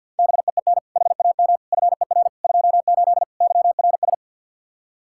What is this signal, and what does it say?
Morse code, heard as a keyed tone, about 50 wpm, 700 Hz. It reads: BEEN HAM FER 28 YRS
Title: K1USN SST QSO Practice 50wpm